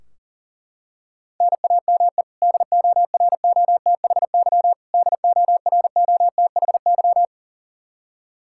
Morse code, heard as a keyed tone, about 40 wpm, 700 Hz. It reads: NAME DOROTHY DOROTHY
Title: Practice QSO - Exchange 1 - Beginning - 40wpm